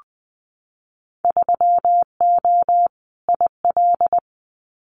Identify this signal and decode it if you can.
Morse code, heard as a keyed tone, about 20 wpm, 700 Hz. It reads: 3O IL
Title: Winter Field Day Exchanges Encore 20wpm